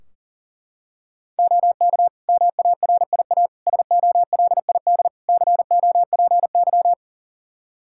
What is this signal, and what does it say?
Morse code, heard as a keyed tone, about 40 wpm, 700 Hz. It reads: OK MARIA SOLID COPY
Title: Practice QSO - Exchange 3 - More Info - 40wpm